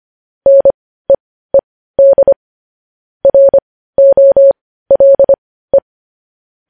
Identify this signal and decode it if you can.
Morse code, heard as a keyed tone, about 25 wpm, spaced as for 15 wpm, 550 Hz. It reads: NEED ROLE
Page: news-headlines-in-morse-code-at-15-wpm